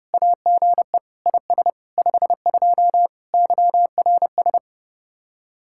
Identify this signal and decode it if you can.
Morse code, heard as a keyed tone, about 36 wpm, 700 Hz. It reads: AGE IS 52 YRS
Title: Speed-Racing Common QSO Phrases Encore - 30wpm